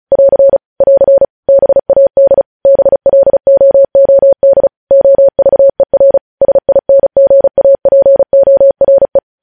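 Morse code, heard as a keyed tone, about 36 wpm, 550 Hz. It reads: <AR> <AR> BAD BLOOD OVER SINGAPORE
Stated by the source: Morse practice at 35 wpm sending news headlines.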